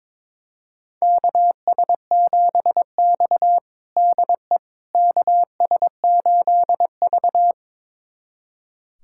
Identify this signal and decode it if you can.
Morse code, heard as a keyed tone, about 22 wpm, 700 Hz. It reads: KS7X DE KS8V
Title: USA Callsign DE Callsign - 4 Characters - 22wpm